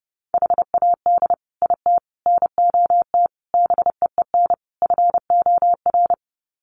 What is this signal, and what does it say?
Morse code, heard as a keyed tone, about 30 wpm, 700 Hz. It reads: HAD IT NOT BEEN FOR
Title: Sets of 5 Words Encore2 30wpm